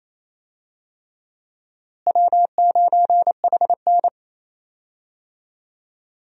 Morse code, heard as a keyed tone, about 27 wpm, 700 Hz. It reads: W9HN
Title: USA Call Signs - 4 Characters - Repeated 3x - 28wpm